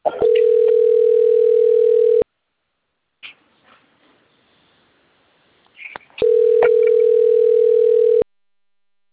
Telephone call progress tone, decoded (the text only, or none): ringback tone